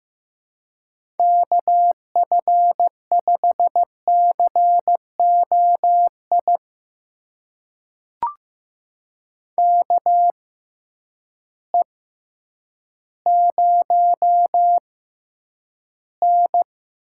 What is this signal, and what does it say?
Morse code, heard as a keyed tone, about 15 wpm, spaced as for 9 wpm, 700 Hz. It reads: KF5COI KE0N
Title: USA Call Signs - 6 Characters - 6x Character Spacing - 15wpm